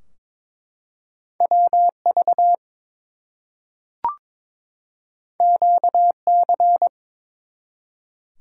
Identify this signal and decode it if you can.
Morse code, heard as a keyed tone, about 22 wpm, 700 Hz. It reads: WV QC
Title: US States and Canadian Provinces 22wpm